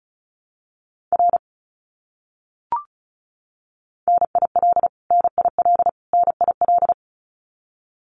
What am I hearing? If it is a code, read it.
Morse code, heard as a keyed tone, about 34 wpm, 700 Hz. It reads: R NIL NIL NIL
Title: Abbreviations Repeated 3x - 35wpm